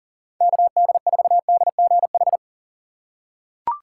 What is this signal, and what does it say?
Morse code, heard as a keyed tone, about 40 wpm, 700 Hz. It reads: KD4DGH
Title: USA Call Signs - 6 Characters - 4x Character Spacing - 40wpm